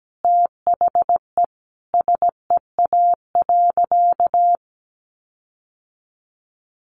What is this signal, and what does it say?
Morse code, heard as a keyed tone, about 17 wpm, 700 Hz. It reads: THE SEA.
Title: Sentences from Top 1300 Words 17wpm